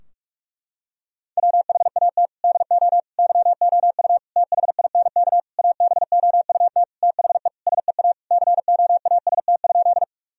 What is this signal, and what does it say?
Morse code, heard as a keyed tone, about 41 wpm, 700 Hz. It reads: WHAT DO YOU THINK ABOUT THE SEA COAST?
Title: Sentences from Top 1000 Words - Review Part 4 - 45wpm